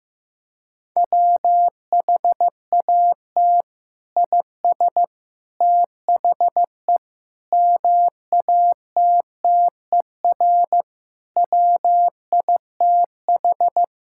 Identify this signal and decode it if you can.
Morse code, heard as a keyed tone, about 15 wpm, 700 Hz. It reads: WHAT IS THE MATTER WITH
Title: Sentences from Top 1000 Words - Review Part 5 - 15wpm